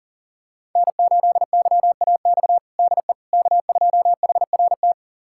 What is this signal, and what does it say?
Morse code, heard as a keyed tone, about 40 wpm, 700 Hz. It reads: N8YAX DE K2HRT
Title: USA Callsign DE Callsign - 5 Characters - 40wpm